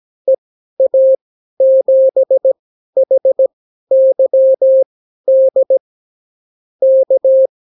Morse code, heard as a keyed tone, about 17 wpm, spaced as for 12 wpm, 530 Hz. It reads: EA7HYD K
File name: CW QSO libroCW4 18-8ppm